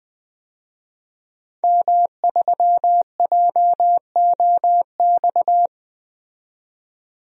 Morse code, heard as a keyed tone, about 20 wpm, 700 Hz. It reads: M3JOX
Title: DXCC-100 Call Signs - 5 Characters - 20wpm